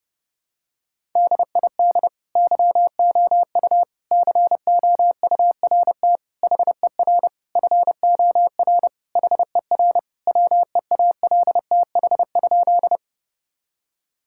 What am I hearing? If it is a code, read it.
Morse code, heard as a keyed tone, about 30 wpm, 700 Hz. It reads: DID YOU COURT HER FOR HER WEALTH?
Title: Sentences from Top 1600 Words 30wpm